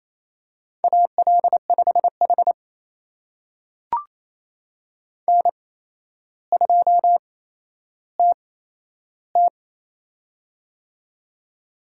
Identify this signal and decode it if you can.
Morse code, heard as a keyed tone, about 28 wpm, spaced as for 12 wpm, 700 Hz. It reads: AL5H N2TT
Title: USA Call Signs - 4 Characters - 8x Character Spacing - 28wpm